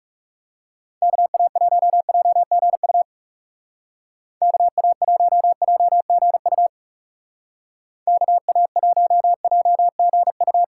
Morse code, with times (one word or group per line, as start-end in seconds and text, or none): 1.0-3.0 KA1JGU
4.4-6.7 KA1JGU
8.1-10.6 KA1JGU